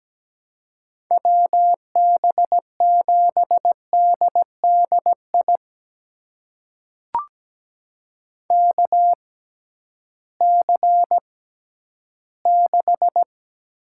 Morse code, heard as a keyed tone, about 17 wpm, spaced as for 11 wpm, 700 Hz. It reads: WB7DDI KC6
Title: USA Call Signs - 6 Characters - 6x Character Spacing - 17wpm